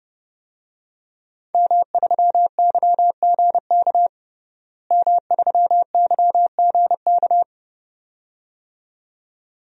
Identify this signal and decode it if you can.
Morse code, heard as a keyed tone, about 30 wpm, 700 Hz. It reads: M3YGK M3YGK
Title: DXCC-100 Call Signs - 5 Characters - Repeated 2x - 30wpm